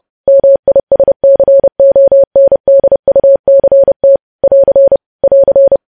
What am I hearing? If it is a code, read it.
Morse code, heard as a keyed tone, about 30 wpm, 550 Hz. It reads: MISCONDUCT <AR> <AR>
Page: News Headlines in Morse Code at 30 WPM